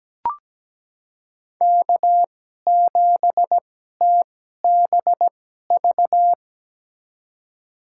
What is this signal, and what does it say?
Morse code, heard as a keyed tone, about 17 wpm, spaced as for 12 wpm, 700 Hz. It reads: K7TBV
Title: USA Call Signs - 5 Characters - 2x Character Spacing - 17wpm